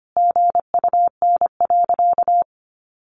Morse code, heard as a keyed tone, about 25 wpm, 700 Hz. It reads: GUN.
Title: A Sound Of Thunder 25wpm